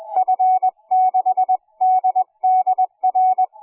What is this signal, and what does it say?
Morse code, heard as a keyed tone, about 20 wpm, 740 Hz. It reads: F6DDR